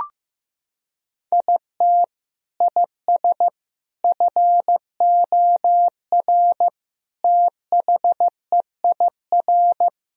Morse code, heard as a keyed tone, about 15 wpm, 700 Hz. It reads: IT IS FOR THEIR
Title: Sentences from Top 1000 Words - Review Part 4 - 15wpm